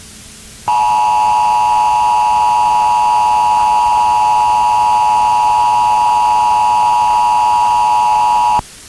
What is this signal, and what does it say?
Emergency Alert System (SAME) tones (the 853 and 960 Hz attention tone), with noise. It is heard from 0.6 to 8.6 s.